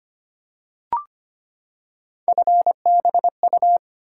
Morse code, heard as a keyed tone, about 25 wpm, 700 Hz. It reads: FBU
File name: Three Letter-Number Combinations - 25wpm